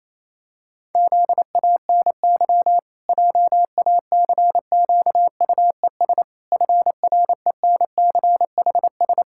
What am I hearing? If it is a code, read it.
Morse code, heard as a keyed tone, about 28 wpm, 700 Hz. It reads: ZANY JACQUES FRENCHS
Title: Pangram Challenge v2 - 28wpm